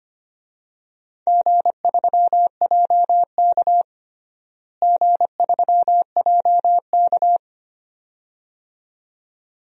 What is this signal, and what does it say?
Morse code, heard as a keyed tone, about 25 wpm, 700 Hz. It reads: G3JK G3JK
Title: DXCC-100 Call Signs - 4 Characters - Repeated 2x - 25wpm